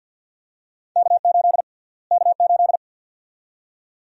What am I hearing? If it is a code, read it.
Morse code, heard as a keyed tone, about 51 wpm, 700 Hz. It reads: K8 K8
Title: USA Call Sign Prefixes - Repeated 2x - 50wpm